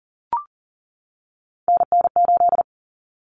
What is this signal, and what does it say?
Morse code, heard as a keyed tone, about 40 wpm, 700 Hz. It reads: NN8
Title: Common Cut Numbers - 3-Digit 40wpm